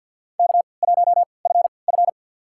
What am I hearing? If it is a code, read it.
Morse code, heard as a keyed tone, about 46 wpm, spaced as for 27 wpm, 700 Hz. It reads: K1FF
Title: USA Call Signs - 3x Character Spacing - 50wpm